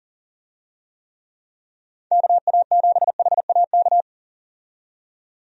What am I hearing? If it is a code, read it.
Morse code, heard as a keyed tone, about 40 wpm, 700 Hz. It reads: KA7HAK
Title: USA Call Signs - 6 Characters - 40wpm